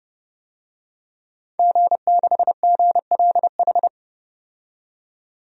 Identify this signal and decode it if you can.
Morse code, heard as a keyed tone, about 30 wpm, 700 Hz. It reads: G6GLH
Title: DXCC-100 Call Signs - 5 Characters - 30wpm